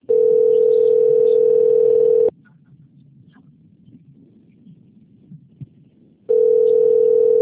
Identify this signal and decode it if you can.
Ringback tone (North America), 440 + 480 Hz.